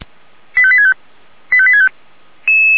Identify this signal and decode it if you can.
Five-tone selective call (ZVEI), heard as quick runs of five tones, tones of about 70 ms. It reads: 85865 85865
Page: ZVEI